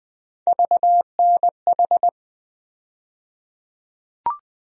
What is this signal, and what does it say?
Morse code, heard as a keyed tone, about 20 wpm, 700 Hz. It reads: VNH
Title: USA Call Sign Postfixes - 3 Letters 20wpm